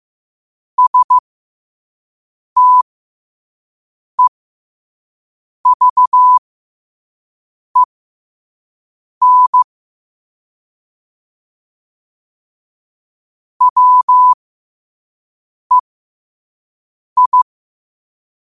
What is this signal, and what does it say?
Morse code, heard as a keyed tone, about 15 wpm, spaced as for 5 wpm, 1000 Hz. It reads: STEVEN WEI
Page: New quotes every day in morse code at 5 Words per minute.